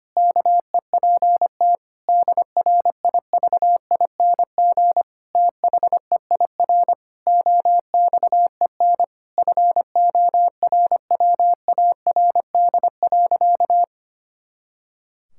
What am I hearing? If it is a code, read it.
Morse code, heard as a keyed tone, about 25 wpm, 700 Hz. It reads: KEPT DRIVING THEIR OXEN FORWARD.
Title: Pangram Challenge v2 - 25wpm